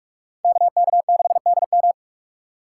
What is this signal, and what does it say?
Morse code, heard as a keyed tone, about 46 wpm, 700 Hz. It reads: KK6DM
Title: USA Call Signs - 4x Character Spacing - 45wpm